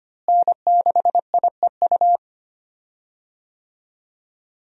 Morse code, heard as a keyed tone, about 25 wpm, 700 Hz. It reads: N6IEU
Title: USA Call Signs - 5 Characters - Repeated 2x - 25wpm